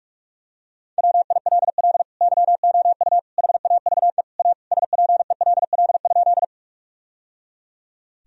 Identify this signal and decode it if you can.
Morse code, heard as a keyed tone, about 41 wpm, 700 Hz. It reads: WILL YOU HAVE A SPELL?
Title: Sentences from Top 1000 Words - Review Part 1 - 45wpm